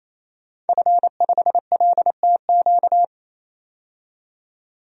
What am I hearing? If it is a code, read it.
Morse code, heard as a keyed tone, about 28 wpm, 700 Hz. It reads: F5LTQ
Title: DXCC-100 Call Signs - 5 Characters - 28wpm